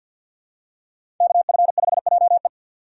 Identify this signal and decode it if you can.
Morse code, heard as a keyed tone, about 49 wpm, 700 Hz. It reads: KF5JE
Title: USA Call Signs - 2x Character Spacing - 50wpm